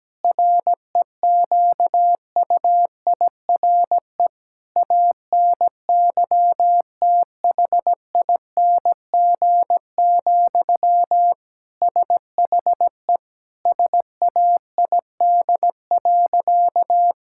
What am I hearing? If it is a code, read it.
Morse code, heard as a keyed tone, about 17 wpm, 700 Hz. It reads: REQUIRE ANYTHING, SHE SAID.
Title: Sentences from Top 1000 Words - Review Part 6 - 17wpm